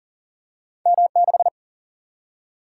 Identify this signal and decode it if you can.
Morse code, heard as a keyed tone, about 40 wpm, 700 Hz. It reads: M6
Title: Single-Letter Single-Number 40wpm